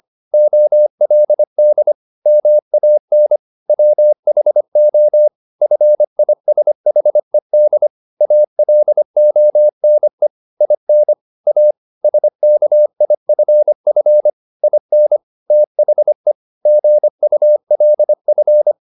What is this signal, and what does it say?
Morse code, heard as a keyed tone, about 25 wpm, 600 Hz. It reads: OLD MAN WHO FISHED ALONE IN A SKIFF IN THE GULF